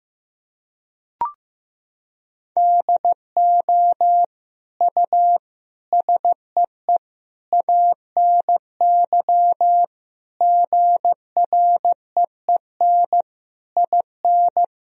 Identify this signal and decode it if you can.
Morse code, heard as a keyed tone, about 15 wpm, 700 Hz. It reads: DO U SEE ANY GREEN IN
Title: Sentences from Top 500 Words and Abbreviations 15wpm